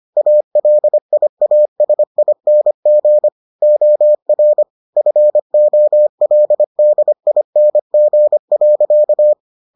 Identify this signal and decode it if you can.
Morse code, heard as a keyed tone, about 25 wpm, 600 Hz. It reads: ALIASING OR FOLDING.